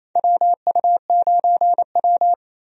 Morse code, heard as a keyed tone, about 28 wpm, 700 Hz. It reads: WU9W
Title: USA Call Signs - 4 Characters - 28wpm